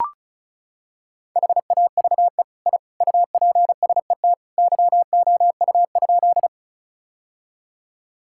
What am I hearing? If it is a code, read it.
Morse code, heard as a keyed tone, about 34 wpm, 700 Hz. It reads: HAVE I UPSET YOU?
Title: Sentences from Top 1000 Words - Review Part 5 - 35wpm